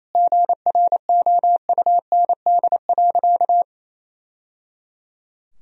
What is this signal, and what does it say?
Morse code, heard as a keyed tone, about 28 wpm, 700 Hz. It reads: GROUND.
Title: Sentences from Top 1000 Words - Review Part 3 - 28wpm